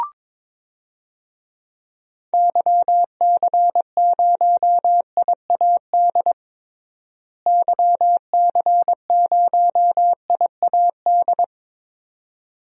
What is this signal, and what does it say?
Morse code, heard as a keyed tone, about 22 wpm, 700 Hz. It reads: YC0IAD YC0IAD
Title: DXCC-100 Call Signs - 6 Characters - Repeated 2x - 22wpm